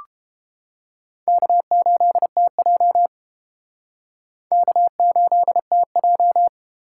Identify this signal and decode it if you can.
Morse code, heard as a keyed tone, about 30 wpm, 700 Hz. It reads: K8TJ K8TJ